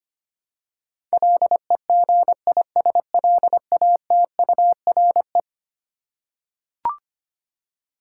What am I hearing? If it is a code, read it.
Morse code, heard as a keyed tone, about 25 wpm, 700 Hz. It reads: LEGISLATURE
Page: morse-code-ninja-practice-25wpm